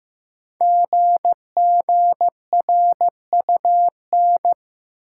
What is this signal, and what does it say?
Morse code, heard as a keyed tone, about 15 wpm, 700 Hz. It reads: GGRUN
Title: Code Groups - 5 Letters - 15wpm